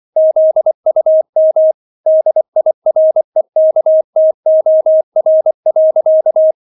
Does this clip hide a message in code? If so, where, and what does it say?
Yes: Morse code from 0.2 to 6.5 s, heard as a keyed tone, about 24 wpm, 630 Hz. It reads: ZUM DIREKTOR.